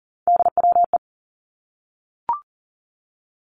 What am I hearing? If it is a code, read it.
Morse code, heard as a keyed tone, about 42 wpm, 700 Hz. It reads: DWE
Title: Three Challenging Letter-Number Combinations 40wpm